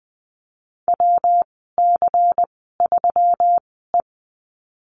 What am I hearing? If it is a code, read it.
Morse code, heard as a keyed tone, about 20 wpm, spaced as for 15 wpm, 700 Hz. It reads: WC3E
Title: 128 Lesson 21g - Number 6 - Call Signs 20wpm